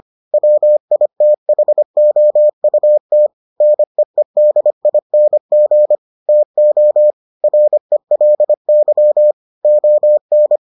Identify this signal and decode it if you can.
Morse code, heard as a keyed tone, about 25 wpm, 600 Hz. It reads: WITHOUT NEEDING TO RELY ON